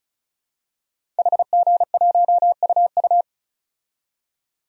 Morse code, heard as a keyed tone, about 34 wpm, 700 Hz. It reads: HG1UU
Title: DXCC-100 Call Signs - 5 Characters - 35wpm